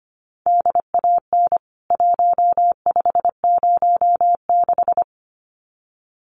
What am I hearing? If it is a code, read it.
Morse code, heard as a keyed tone, about 25 wpm, 700 Hz. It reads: DAN 1506
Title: CWT Exchange 25wpm